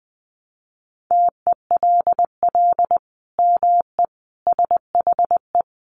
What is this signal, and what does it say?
Morse code, heard as a keyed tone, about 20 wpm, 700 Hz. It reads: TELL ME SHE
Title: Sentences from Top 1200 Words 20wpm